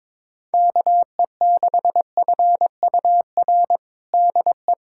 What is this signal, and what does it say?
Morse code, heard as a keyed tone, about 22 wpm, 700 Hz. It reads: KE6FUR DE
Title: USA Callsign DE Callsign 22wpm